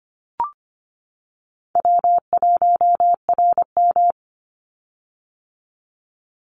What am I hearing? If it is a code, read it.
Morse code, heard as a keyed tone, about 24 wpm, 700 Hz. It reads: W1RM
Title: Call Signs - 25wpm